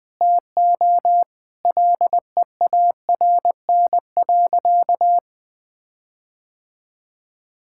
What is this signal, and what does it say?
Morse code, heard as a keyed tone, about 20 wpm, 700 Hz. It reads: TO LEARN.
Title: Sentences with 8-Word Phrases - 20wpm